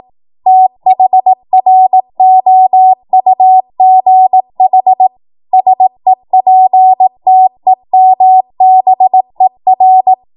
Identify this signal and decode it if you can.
Morse code, heard as a keyed tone, about 18 wpm, 750 Hz. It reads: THROUGH SEPTEMBER
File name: Morse / 18WPM